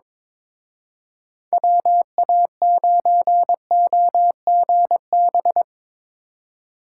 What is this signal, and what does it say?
Morse code, heard as a keyed tone, about 24 wpm, 700 Hz. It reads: WA9OGB
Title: Speed Racing - USA 2x3 Call Signs 22wpm